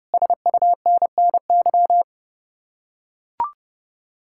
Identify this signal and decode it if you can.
Morse code, heard as a keyed tone, about 31 wpm, 700 Hz. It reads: SUNNY